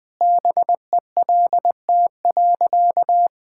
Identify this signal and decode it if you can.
Morse code, heard as a keyed tone, about 21 wpm, 700 Hz. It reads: BELT.
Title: Easy Sentences with Hints - 20wpm